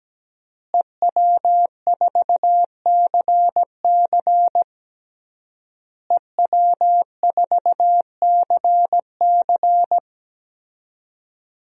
Morse code, heard as a keyed tone, about 17 wpm, 700 Hz. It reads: EW4CC EW4CC
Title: DXCC-100 Call Signs - 5 Characters - Repeated 3x - 17wpm